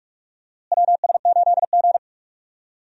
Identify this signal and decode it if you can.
Morse code, heard as a keyed tone, about 46 wpm, 700 Hz. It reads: WS8G
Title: USA Call Signs - 4 Characters - Repeated 3x - 45wpm